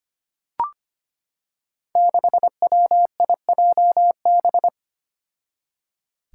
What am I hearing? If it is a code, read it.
Morse code, heard as a keyed tone, about 25 wpm, 700 Hz. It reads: 6WIJB
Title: Five Challenging Letter-Number Combinations 25wpm